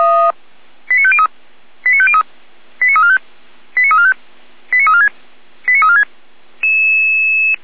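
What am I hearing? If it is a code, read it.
Five-tone selective call (ZVEI), heard as quick runs of five tones, tones of about 70 ms. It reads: 89503 89503 89345 89345 89346 89346
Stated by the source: ZVEI five-tone selective call received on SDR, from feuerwehr-orlamuende fire brigade alerting.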